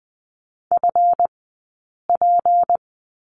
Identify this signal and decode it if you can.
Morse code, heard as a keyed tone, about 20 wpm, spaced as for 8 wpm, 700 Hz. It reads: FP
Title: 165 Lesson 27d - Number 7 - Two Characters 20wpm